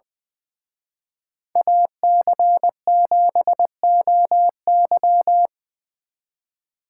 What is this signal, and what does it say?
Morse code, heard as a keyed tone, about 23 wpm, 700 Hz. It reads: AC7OY
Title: Speed Racing - USA 2x2 Call Signs 20wpm